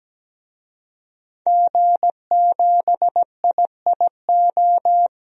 Morse code, heard as a keyed tone, about 17 wpm, 700 Hz. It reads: G7IIO
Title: DXCC-100 Call Signs - 5 Characters - 17wpm